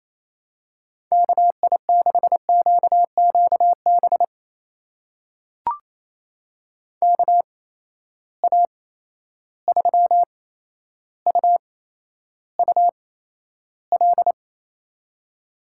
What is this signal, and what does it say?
Morse code, heard as a keyed tone, about 28 wpm, spaced as for 12 wpm, 700 Hz. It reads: KI6QQB KA3UUL
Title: USA Call Signs - 8x Character Spacing - 28wpm